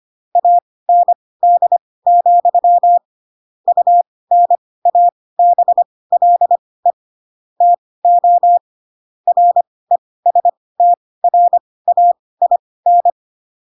Morse code, heard as a keyed tone, about 25 wpm, spaced as for 18 wpm, 700 Hz. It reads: AND, UNABLE TO RESTRAIN